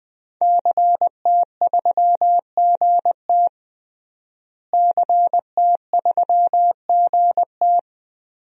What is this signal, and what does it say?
Morse code, heard as a keyed tone, about 20 wpm, 700 Hz. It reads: CT3GT CT3GT
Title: DXCC-100 Call Signs - 5 Characters - Repeated 2x - 20wpm